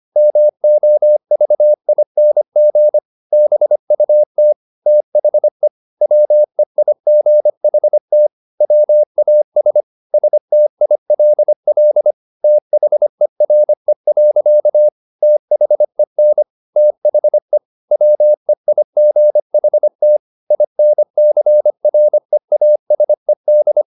Morse code, heard as a keyed tone, about 25 wpm, 600 Hz. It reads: MOVING BUT THE WEIGHT WAS STILL THERE. THEN THE WEIGHT INCREASED
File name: the old man and the sea part 1 25wpm